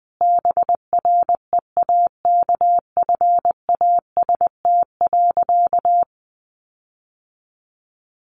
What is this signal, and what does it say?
Morse code, heard as a keyed tone, about 20 wpm, 700 Hz. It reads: BREAKFAST.
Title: Sentences from Top 1500 Words 20wpm